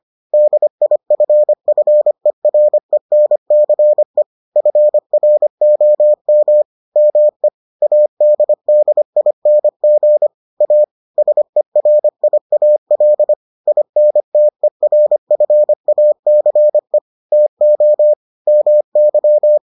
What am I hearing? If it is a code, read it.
Morse code, heard as a keyed tone, about 25 wpm, 600 Hz. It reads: DIFFERENCE FROM ME ADDING A SERIAL INTERFACE TO MY